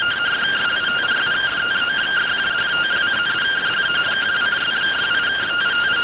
Two-tone teleprinter signal, about 170 Hz shift.